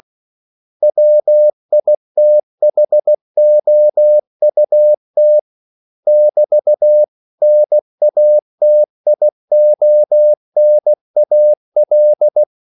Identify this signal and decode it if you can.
Morse code, heard as a keyed tone, about 16 wpm, 600 Hz. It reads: WITHOUT <BT>NATIONAL
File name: CW / Practice / 500MostCommonEnglishWords16